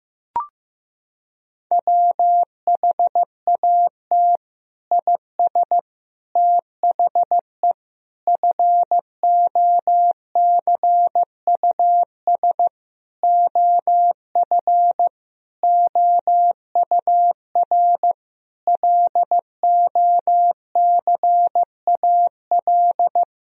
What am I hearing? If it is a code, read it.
Morse code, heard as a keyed tone, about 15 wpm, 700 Hz. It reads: WHAT IS THE FOCUS OF OUR LOCAL
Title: Sentences from Top 1000 Words - Review Part 4 - 15wpm